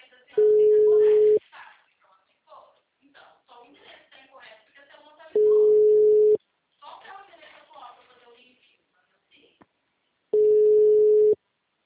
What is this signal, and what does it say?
Ringback tone (Europe, 425 Hz), with speech.